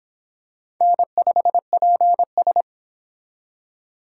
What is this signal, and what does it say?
Morse code, heard as a keyed tone, about 26 wpm, 700 Hz. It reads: N5PS